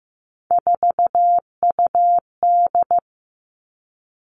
Morse code, heard as a keyed tone, about 15 wpm, 700 Hz. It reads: 4UD
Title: All Cut Numbers - 3-Digit 15wpm